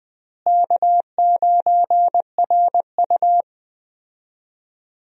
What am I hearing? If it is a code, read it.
Morse code, heard as a keyed tone, about 21 wpm, 700 Hz. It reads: K9RU
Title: USA Call Signs - 4 Characters - Repeated 2x - 20wpm